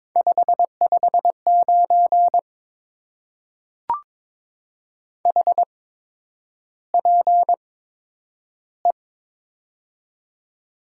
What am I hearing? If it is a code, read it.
Morse code, heard as a keyed tone, about 22 wpm, spaced as for 9 wpm, 700 Hz. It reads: 559 HPE
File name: Top 90 QSO Elements - 8x Character Spacing - 22wpm